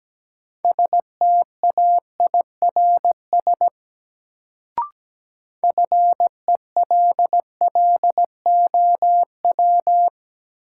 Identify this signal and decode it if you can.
Morse code, heard as a keyed tone, about 17 wpm, 700 Hz. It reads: STAIRS FELLOW
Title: Sets of 5 Words Encore - New Word Drill - 17wpm